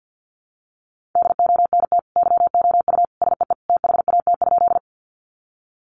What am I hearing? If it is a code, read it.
Morse code, heard as a keyed tone, about 50 wpm, 700 Hz. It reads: DONT YOU SEE THAT?
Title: Sentences from Top 300 Words 50wpm